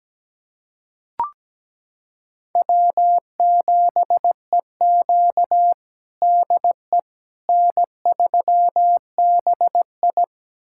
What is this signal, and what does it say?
Morse code, heard as a keyed tone, about 17 wpm, 700 Hz. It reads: W7EQ DE N3BI
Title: USA Callsign DE Callsign - 4 Characters - 17wpm